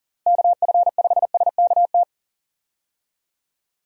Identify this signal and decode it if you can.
Morse code, heard as a keyed tone, about 40 wpm, 700 Hz. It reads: KF5SKT
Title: USA Call Signs - 6 Characters - Repeated 2x - 40wpm